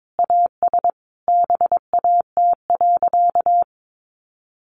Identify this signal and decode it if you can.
Morse code, heard as a keyed tone, about 22 wpm, 700 Hz. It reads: AS BAT.
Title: Sentences from Top 1500 Words 22wpm